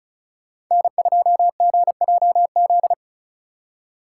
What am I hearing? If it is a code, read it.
Morse code, heard as a keyed tone, about 44 wpm, 700 Hz. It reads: N2GJZ